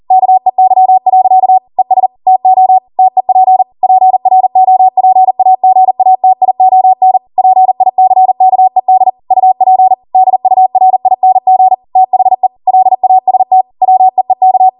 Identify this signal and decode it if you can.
Morse code, heard as a keyed tone, about 40 wpm, 750 Hz. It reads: KEY. ES TO TEP PROPAGATION PICKED UP DURING THE LAST WEEK